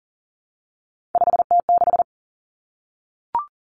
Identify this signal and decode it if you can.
Morse code, heard as a keyed tone, about 40 wpm, 700 Hz. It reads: 5T6
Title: Three Challenging Letter-Number Combinations 40wpm